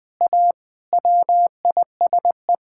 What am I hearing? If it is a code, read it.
Morse code, heard as a keyed tone, about 20 wpm, 700 Hz. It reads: A WISE
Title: Sentences from Top 1600 Words 20wpm